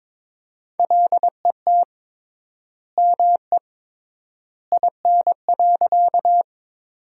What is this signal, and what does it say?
Morse code, heard as a keyed tone, about 22 wpm, 700 Hz. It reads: LET ME IN.
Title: 3x Word-Spacing - Sentences from Top 300 Words - 22wpm